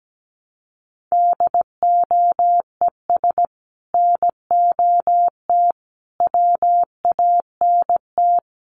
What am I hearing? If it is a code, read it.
Morse code, heard as a keyed tone, about 17 wpm, 700 Hz. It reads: DOES NOT WANT
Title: Sets of 5 Words Encore 17wpm